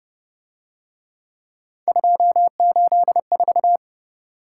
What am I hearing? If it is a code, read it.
Morse code, heard as a keyed tone, about 30 wpm, 700 Hz. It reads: 284